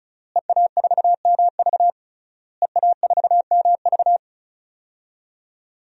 Morse code, heard as a keyed tone, about 36 wpm, 700 Hz. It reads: EA4MV EA4MV
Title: DXCC-100 Call Signs - 5 Characters - Repeated 3x - 35wpm